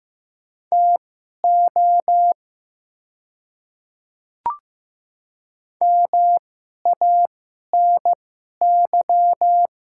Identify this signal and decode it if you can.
Morse code, heard as a keyed tone, about 15 wpm, spaced as for 11 wpm, 700 Hz. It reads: TO MANY
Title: 2x Character Spacing - Top 100 Words - 15wpm